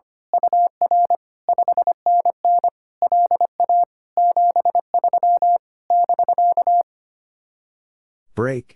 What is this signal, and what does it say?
Morse code, heard as a keyed tone, about 24 wpm, 700 Hz. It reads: UR 5NN LA 73 <BK>
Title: Speed-Racing Common QSO Phrases - 25wpm